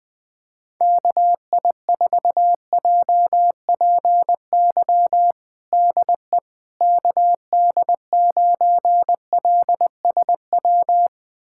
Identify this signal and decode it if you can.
Morse code, heard as a keyed tone, about 20 wpm, 700 Hz. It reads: KI4JPY DE KD9LSW
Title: USA Callsign DE Callsign 20wpm